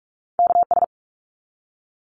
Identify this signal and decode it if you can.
Morse code, heard as a keyed tone, about 44 wpm, 700 Hz. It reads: KS
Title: ARRL Sections 45wpm